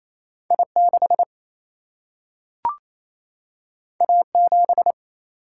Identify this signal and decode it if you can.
Morse code, heard as a keyed tone, about 28 wpm, 700 Hz. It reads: I6 A7
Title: Single-Letter Single-Number 28wpm